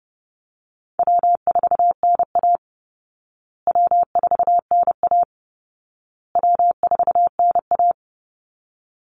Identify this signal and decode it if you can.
Morse code, heard as a keyed tone, about 30 wpm, 700 Hz. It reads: W4NA W4NA W4NA
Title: Easy Call Signs Repeated 3x - 30wpm